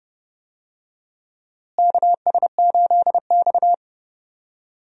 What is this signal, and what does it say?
Morse code, heard as a keyed tone, about 30 wpm, 700 Hz. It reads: KS8X
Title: USA Call Signs - 4 Characters - Repeated 3x - 30wpm